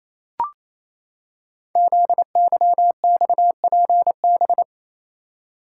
Morse code, heard as a keyed tone, about 27 wpm, 700 Hz. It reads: ZYXPB
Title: Code Groups - 5 Letter:Numbers - 28wpm